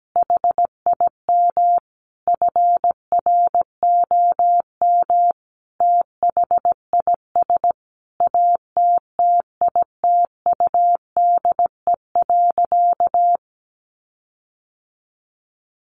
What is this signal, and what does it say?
Morse code, heard as a keyed tone, about 17 wpm, 700 Hz. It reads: HIM FROM THIS ATTITUDE.
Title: Sentences from Top 1300 Words 17wpm